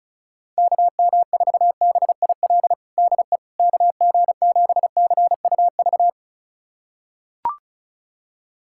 Morse code, heard as a keyed tone, about 36 wpm, 700 Hz. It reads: KM4BIL DE KG7CUV
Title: USA Callsign DE Callsign - 6 Characters - 35wpm